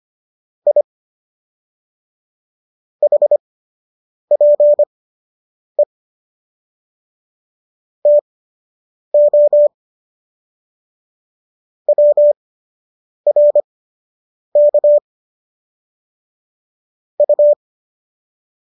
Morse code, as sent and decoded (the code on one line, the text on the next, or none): .. / .... .--. / - --- / .-- .-. -.- / ..-
I HPE TO WRK U